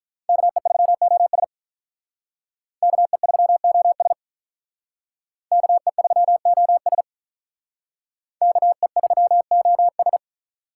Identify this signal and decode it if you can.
Morse code, heard as a keyed tone, about 45 wpm, 700 Hz. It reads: KE3OS KE3OS KE3OS KE3OS